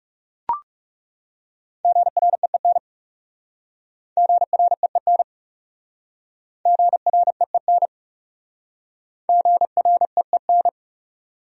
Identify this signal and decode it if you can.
Morse code, heard as a keyed tone, about 38 wpm, 700 Hz. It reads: GREEN GREEN GREEN GREEN